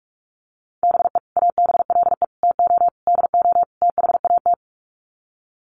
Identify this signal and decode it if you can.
Morse code, heard as a keyed tone, about 44 wpm, 700 Hz. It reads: BE ABLE TO DO THAT
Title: Sets of 5 Words Encore2 45wpm